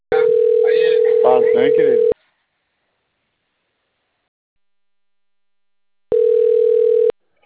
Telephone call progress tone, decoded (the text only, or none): ringback tone